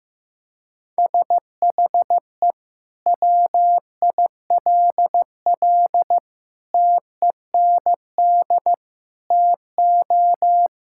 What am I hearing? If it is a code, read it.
Morse code, heard as a keyed tone, about 15 wpm, 700 Hz. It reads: SHE WILL TEND TO
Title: Sentences from Top 1000 Words - Review Part 5 - 15wpm